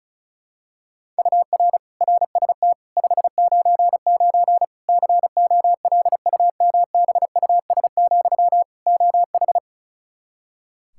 Morse code, heard as a keyed tone, about 36 wpm, 700 Hz. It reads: UR RST 599 COLUMBUS, OH
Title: K1USN SST QSO Practice 35wpm